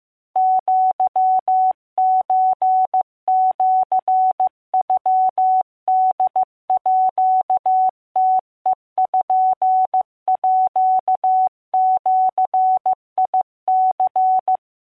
Morse code (Japanese Wabun code), heard as a keyed tone, about 15 wpm, 750 Hz. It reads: アソシノホームページニ